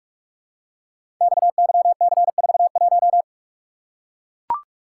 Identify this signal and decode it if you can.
Morse code, heard as a keyed tone, about 44 wpm, 690 Hz. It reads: XYC41
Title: Code Groups - 5 Letter:Numbers - 45wpm